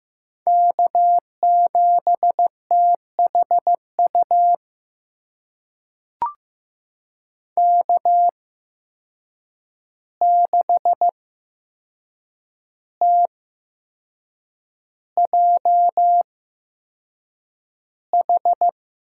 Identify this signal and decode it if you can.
Morse code, heard as a keyed tone, about 15 wpm, spaced as for 6 wpm, 700 Hz. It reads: K7THU K6TJH
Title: USA Call Signs - 5 Characters - 8x Character Spacing - 15wpm